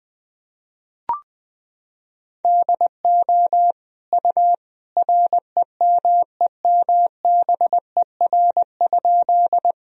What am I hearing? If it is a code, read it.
Morse code, heard as a keyed tone, about 21 wpm, 700 Hz. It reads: DO U REMEMBER?
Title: Sentences from Top 400 Words and Abbreviations 20wpm